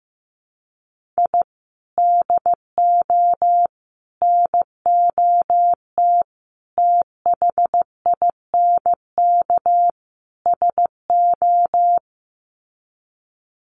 Morse code, heard as a keyed tone, about 15 wpm, 700 Hz. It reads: I DO NOT THINK SO
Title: Sets of 5 Words 15wpm